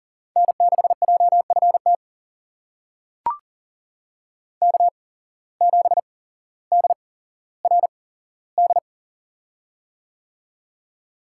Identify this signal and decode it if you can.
Morse code, heard as a keyed tone, about 40 wpm, spaced as for 17 wpm, 700 Hz. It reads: N6JFT K7DRD
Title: USA Call Signs - 5 Characters - 8x Character Spacing - 40wpm